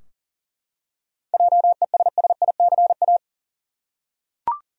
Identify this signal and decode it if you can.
Morse code, heard as a keyed tone, about 40 wpm, 700 Hz. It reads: JESSICA
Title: USA Common Names - 40wpm